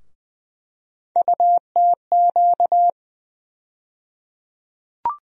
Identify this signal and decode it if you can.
Morse code, heard as a keyed tone, about 20 wpm, 700 Hz. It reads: UTQ
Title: USA Call Sign Postfixes - 3 Letters 20wpm